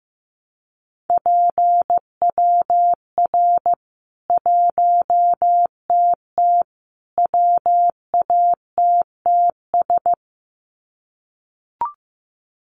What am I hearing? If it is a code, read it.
Morse code, heard as a keyed tone, about 15 wpm, 700 Hz. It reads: PWR 1TT WATTS
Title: Common QSO Phrases 15wpm